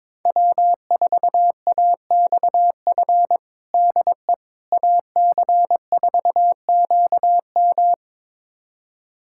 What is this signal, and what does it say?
Morse code, heard as a keyed tone, about 22 wpm, 700 Hz. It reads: W4AXF DE AC4QM
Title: USA Callsign DE Callsign - 5 Characters - 22wpm